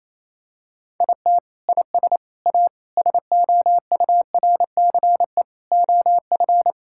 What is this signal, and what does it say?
Morse code, heard as a keyed tone, about 28 wpm, 700 Hz. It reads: IT IS A SOURCE OF
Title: Sentences from Top 1000 Words - Review Part 5 - 28wpm